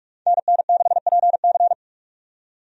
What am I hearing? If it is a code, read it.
Morse code, heard as a keyed tone, about 46 wpm, 700 Hz. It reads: NN6PC
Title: USA Call Signs - 5 Characters - 45wpm